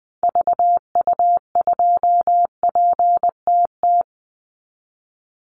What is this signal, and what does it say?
Morse code, heard as a keyed tone, about 20 wpm, 700 Hz. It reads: VU2PTT
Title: Call Signs Repeated 3x - 20wpm